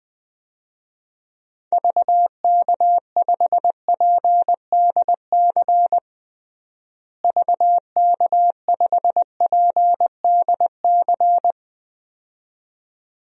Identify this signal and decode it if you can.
Morse code, heard as a keyed tone, about 20 wpm, 700 Hz. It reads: VK5PDC VK5PDC
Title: DXCC-100 Call Signs - 6 Characters - Repeated 2x - 20wpm